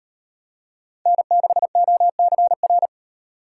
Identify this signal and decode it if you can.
Morse code, heard as a keyed tone, about 38 wpm, 700 Hz. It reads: N6OCR